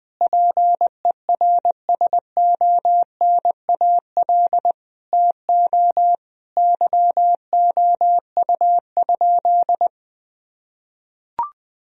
Morse code, heard as a keyed tone, about 21 wpm, 700 Hz. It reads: PERSONAL TO YOU?
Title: Sentences from Top 1000 Words - Review Part 4 - 20wpm